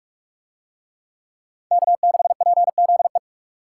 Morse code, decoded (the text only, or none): K6PZE